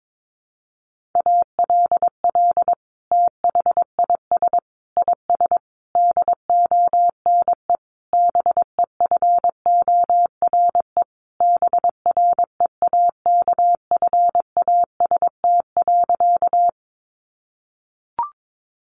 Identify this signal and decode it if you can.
Morse code, heard as a keyed tone, about 22 wpm, 700 Hz. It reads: ALL THIS IS DONE BEFORE BREAKFAST.
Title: Sentences from Top 1500 Words 22wpm